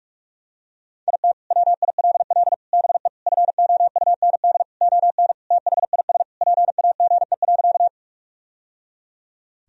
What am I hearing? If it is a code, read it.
Morse code, heard as a keyed tone, about 46 wpm, 700 Hz. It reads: IT WILL BE FOUND ON THIS PAGE.
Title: Sentences from Top 1000 Words - Review Part 1 - 45wpm